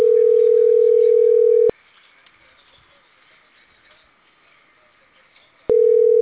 Ringback tone (North America), 440 + 480 Hz.